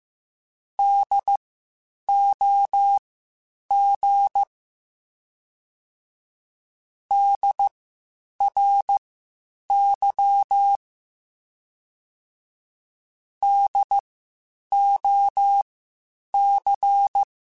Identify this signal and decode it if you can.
Morse code, heard as a keyed tone, about 15 wpm, spaced as for 8 wpm, 800 Hz. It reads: DOG DRY DOC
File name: Spaced Alphas / Spaced Alphas mp3s / Ds sp @13